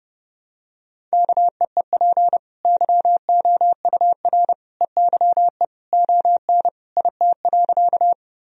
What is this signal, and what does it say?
Morse code, heard as a keyed tone, about 30 wpm, 700 Hz. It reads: KEEP YOUR EYE ON IT.
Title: Sentences from Top 1000 Words - Review Part 1 - 30wpm